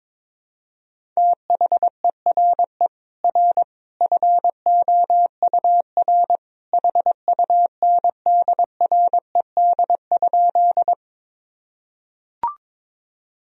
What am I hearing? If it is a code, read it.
Morse code, heard as a keyed tone, about 22 wpm, 700 Hz. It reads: THERE R FOUR HUNDRED?
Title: Sentences from Top 400 Words and Abbreviations 22wpm